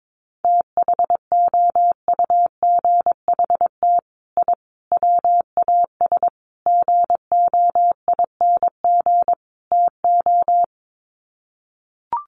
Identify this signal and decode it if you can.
Morse code, heard as a keyed tone, about 22 wpm, 700 Hz. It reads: THOUGHT I WAS GOING TO
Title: Sets of 5 Words Encore2 22wpm